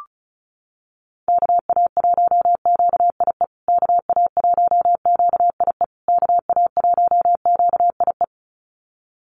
Morse code, heard as a keyed tone, about 34 wpm, 700 Hz. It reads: KA1QIE KA1QIE KA1QIE
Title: USA Call Signs Repeated 3x 35wpm